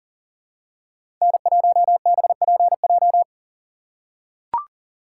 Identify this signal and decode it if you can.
Morse code, heard as a keyed tone, about 40 wpm, 700 Hz. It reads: N1BPJ
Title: USA Call Signs - 2x Character Spacing - 40wpm